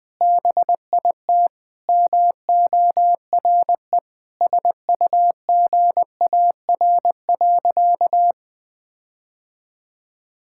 Morse code, heard as a keyed tone, about 19 wpm, 700 Hz. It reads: BIT MORE SUGAR.
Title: Sentences with 4-Word Phrases - 20wpm